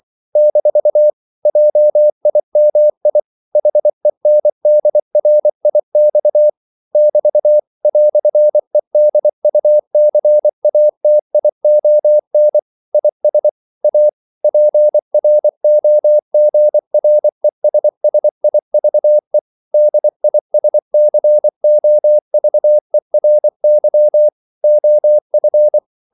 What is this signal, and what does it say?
Morse code, heard as a keyed tone, about 24 wpm, 600 Hz. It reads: - JIMI HENDRIX <BT> "EDUCATION IS A PROGRESSIVE DISCOVERY OF
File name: CW / Practice / EnglishQuotations0224